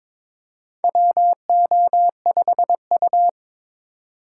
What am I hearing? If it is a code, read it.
Morse code, heard as a keyed tone, about 21 wpm, 700 Hz. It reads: WO5U
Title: USA Call Signs - 4 Characters - 4x Character Spacing - 22wpm